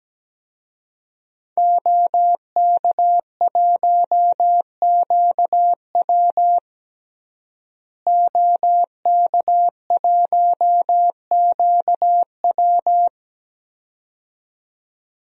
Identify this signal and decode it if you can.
Morse code, heard as a keyed tone, about 17 wpm, 700 Hz. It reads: OK1QW OK1QW
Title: DXCC-100 Call Signs - 5 Characters - Repeated 2x - 17wpm